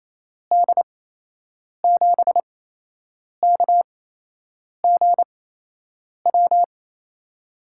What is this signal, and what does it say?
Morse code, heard as a keyed tone, about 28 wpm, spaced as for 8 wpm, 700 Hz. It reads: D7KGW
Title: USA Call Signs - 8x Character Spacing - 28wpm